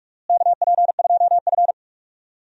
Morse code, heard as a keyed tone, about 46 wpm, 700 Hz. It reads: KP2F
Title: DXCC-100 Call Signs - 4 Characters - Repeated 3x - 45wpm